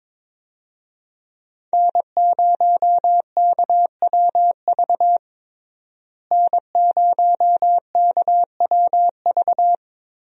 Morse code, heard as a keyed tone, about 22 wpm, 700 Hz. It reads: N0KWV N0KWV
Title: USA Call Signs - 5 Characters - Repeated 3x - 22wpm